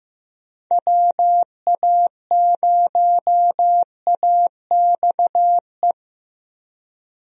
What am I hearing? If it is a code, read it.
Morse code, heard as a keyed tone, about 15 wpm, 700 Hz. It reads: WA0AXE
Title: USA Call Signs - 6 Characters - Repeated 2x - 15wpm